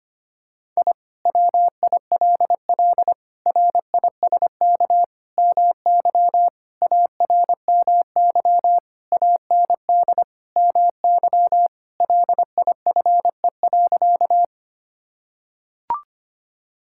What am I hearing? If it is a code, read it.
Morse code, heard as a keyed tone, about 25 wpm, 700 Hz. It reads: I WILL RISK MY ARMY AND MY LIFE.
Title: Sentences from Top 1000 Words - Review Part 5 - 25wpm